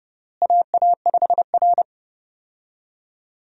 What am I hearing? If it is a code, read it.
Morse code, heard as a keyed tone, about 30 wpm, 700 Hz. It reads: AA5R
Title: USA Call Signs - 4 Characters - Repeated 3x - 30wpm